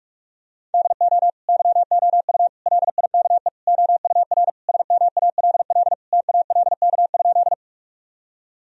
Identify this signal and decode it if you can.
Morse code, heard as a keyed tone, about 46 wpm, 700 Hz. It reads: DO YOU LIKE OUR SMALL TALK?
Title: Sentences from Top 1000 Words - Review Part 1 - 45wpm